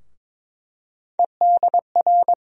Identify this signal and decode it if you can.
Morse code, heard as a keyed tone, about 22 wpm, 700 Hz. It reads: EDR
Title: USA Call Sign Postfixes - 3 Letters 22wpm